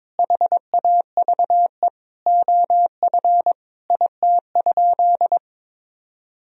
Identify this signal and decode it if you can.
Morse code, heard as a keyed tone, about 22 wpm, 700 Hz. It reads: HAVE OF IT?
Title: Sentences from Top 1000 Words - Review Part 3 - 22wpm